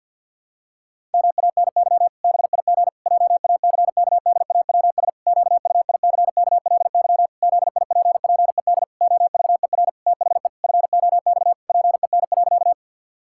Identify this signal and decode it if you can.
Morse code, heard as a keyed tone, about 51 wpm, 700 Hz. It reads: MANY BIG JACKDAWS QUICKLY ZIPPED OVER THE FOX PEN.
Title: Pangram Challenge v2 - 50wpm